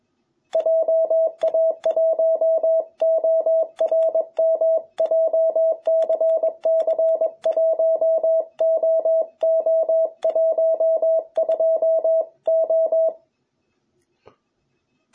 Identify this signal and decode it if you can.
Morse code, heard as a keyed tone, about 21 wpm, 660 Hz. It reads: JA1ORM JCC1OO12O
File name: jh8cbh / 2021cw